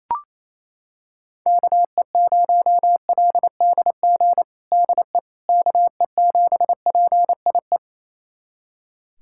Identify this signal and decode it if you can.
Morse code, heard as a keyed tone, about 28 wpm, 700 Hz. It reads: KE0LDG DE KE7PIE
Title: USA Callsign DE Callsign - 6 Characters - 28wpm